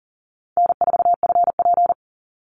Morse code, heard as a keyed tone, about 40 wpm, 700 Hz. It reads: N4FP
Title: Easy Call Signs Repeated 2x - 40wpm